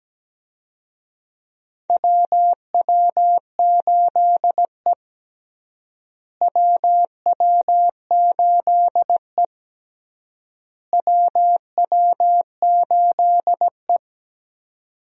Morse code, heard as a keyed tone, about 17 wpm, 700 Hz. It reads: WW8E WW8E WW8E
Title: USA Call Signs - 4 Characters - Repeated 3x - 17wpm